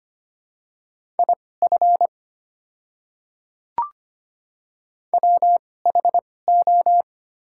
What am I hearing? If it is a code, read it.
Morse code, heard as a keyed tone, about 25 wpm, spaced as for 18 wpm, 690 Hz. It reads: IF WHO
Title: 2x Character Spacing - Top 200 Words - 25wpm